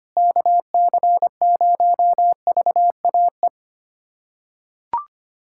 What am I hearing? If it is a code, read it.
Morse code, heard as a keyed tone, about 25 wpm, 700 Hz. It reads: KC0VAE
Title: USA Callsign DE Callsign - 6 Characters - 25wpm